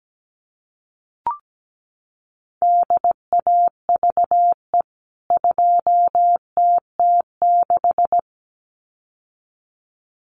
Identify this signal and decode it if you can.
Morse code, heard as a keyed tone, about 17 wpm, 700 Hz. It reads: DAVE 2TT6
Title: CWT Exchange 17wpm